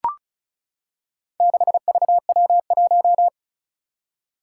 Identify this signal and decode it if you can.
Morse code, heard as a keyed tone, about 36 wpm, 700 Hz. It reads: 6VW1
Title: Four Challenging Letter-Number Combinations 35wpm